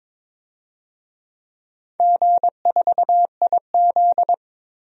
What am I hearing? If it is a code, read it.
Morse code, heard as a keyed tone, about 22 wpm, 700 Hz. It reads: G4IZ
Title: DXCC-100 Call Signs - 4 Characters - 22wpm